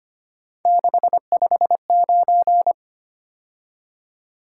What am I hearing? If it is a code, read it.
Morse code, heard as a keyed tone, about 29 wpm, 700 Hz. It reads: 659
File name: Speed-Racing - Three Letter-Number Combinations - 25wpm